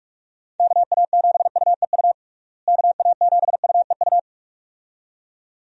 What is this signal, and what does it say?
Morse code, heard as a keyed tone, about 46 wpm, 700 Hz. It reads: KA7UEU KA7UEU
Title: USA Call Signs - 6 Characters - Repeated 2x - 45wpm